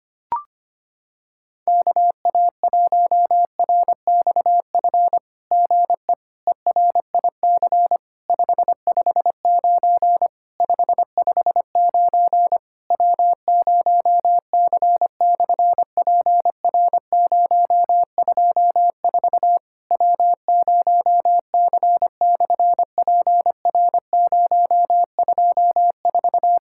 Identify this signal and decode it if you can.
Morse code, heard as a keyed tone, about 25 wpm, 700 Hz. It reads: KA1RXF GE ERIC 559 559 W0C/PR024 W0C/PR024
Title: SOTA QSO Practice 25wpm